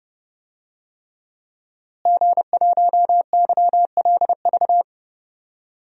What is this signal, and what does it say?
Morse code, heard as a keyed tone, about 30 wpm, 700 Hz. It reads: G1YLV
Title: DXCC-100 Call Signs - 5 Characters - Repeated 3x - 30wpm